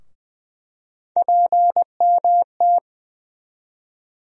Morse code, heard as a keyed tone, about 20 wpm, 700 Hz. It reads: PMT
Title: USA Call Sign Postfixes - 3 Letters 20wpm